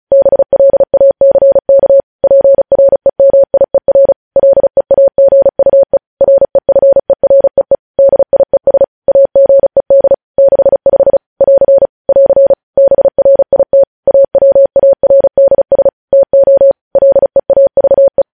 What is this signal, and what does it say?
Morse code, heard as a keyed tone, about 36 wpm, 550 Hz. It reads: BLACK PREMIER LEAGUE REFEREE DIES AGED 65 <AR> <AR> BRIT AWARDS TO LEAVE
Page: news headlines in morse code at 35 wpm